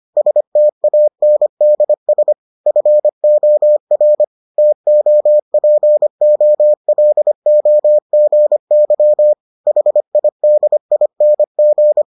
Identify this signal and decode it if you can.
Morse code, heard as a keyed tone, about 25 wpm, 600 Hz. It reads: STANDS FOR TOPOLOGY HIDING